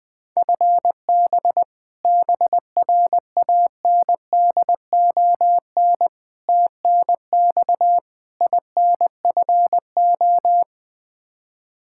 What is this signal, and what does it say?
Morse code, heard as a keyed tone, about 20 wpm, 700 Hz. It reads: FB BRANDON TNX INFO
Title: Practice QSO - Exchange 3 - More Info - 20wpm